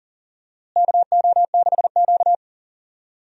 Morse code, heard as a keyed tone, about 39 wpm, 700 Hz. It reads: KO6Q
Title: USA Call Signs - 4 Characters - Repeated 2x - 40wpm